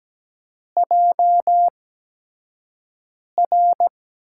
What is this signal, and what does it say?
Morse code, heard as a keyed tone, about 17 wpm, spaced as for 5 wpm, 700 Hz. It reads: JR
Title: USA Call Signs - 4 Characters - 8x Character Spacing - 17wpm